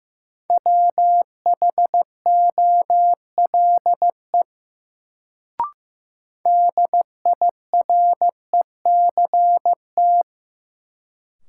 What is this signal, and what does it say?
Morse code, heard as a keyed tone, about 16 wpm, 700 Hz. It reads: WHOLE DIRECT
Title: Sentences from Top 400 Words - New Word Drill - 15wpm